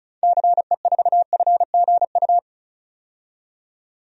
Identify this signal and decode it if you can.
Morse code, heard as a keyed tone, about 36 wpm, 700 Hz. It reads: CE4FGU
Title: DXCC-100 Call Signs - 6 Characters - 35wpm